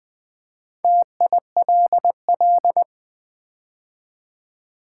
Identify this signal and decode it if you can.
Morse code, heard as a keyed tone, about 20 wpm, 700 Hz. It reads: TILL
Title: Build A Word - 2 Letters At A Time - 4-Letter Words - 20wpm